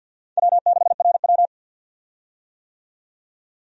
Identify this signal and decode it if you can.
Morse code, heard as a keyed tone, about 52 wpm, 700 Hz. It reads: W6RW
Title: USA Call Signs - 4 Characters - 50wpm